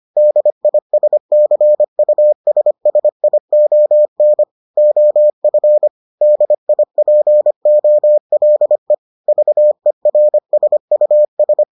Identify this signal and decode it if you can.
Morse code, heard as a keyed tone, about 25 wpm, 600 Hz. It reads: DISCUSSION OF DIPOLE VERSUS